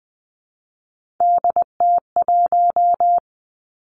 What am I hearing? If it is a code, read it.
Morse code, heard as a keyed tone, about 20 wpm, 700 Hz. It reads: DT1
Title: Three Challenging Letter-Number Combinations 20wpm